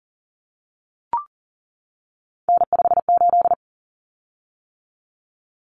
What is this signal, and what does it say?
Morse code, heard as a keyed tone, about 40 wpm, 700 Hz. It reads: N58
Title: Common Cut Numbers - 3-Digit 40wpm